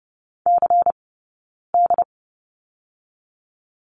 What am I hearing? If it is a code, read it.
Morse code, heard as a keyed tone, about 30 wpm, 700 Hz. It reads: C D
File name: 092 Lesson 16d - Letter c - Two Characters 30wpm